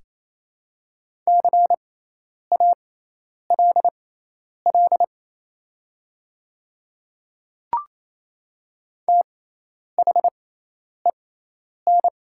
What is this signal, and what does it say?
Morse code, heard as a keyed tone, about 27 wpm, spaced as for 10 wpm, 700 Hz. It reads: CALL THEN
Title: 6x Character Spacing - Top 100 Words - 28wpm